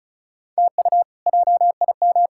Morse code, heard as a keyed tone, about 36 wpm, 700 Hz. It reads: TU JIM
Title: K1USN SST QSO Practice - 35wpm